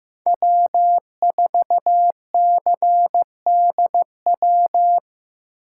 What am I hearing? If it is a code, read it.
Morse code, heard as a keyed tone, about 15 wpm, 700 Hz. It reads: W4CDW
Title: USA Call Signs - 5 Characters - 4x Character Spacing - 15wpm